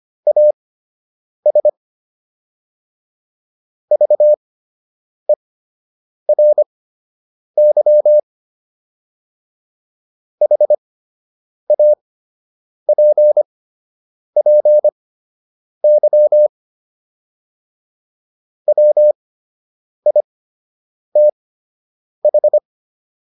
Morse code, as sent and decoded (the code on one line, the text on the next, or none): .- ... / ...- . .-. -.-- / .... .- .--. .--. -.-- / .-- .. - ....
AS VERY HAPPY WITH